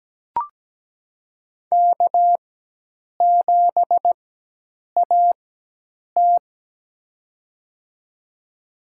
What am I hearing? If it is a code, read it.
Morse code, heard as a keyed tone, about 17 wpm, spaced as for 8 wpm, 700 Hz. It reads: K7AT
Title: USA Call Signs - 4 Characters - 4x Character Spacing - 17wpm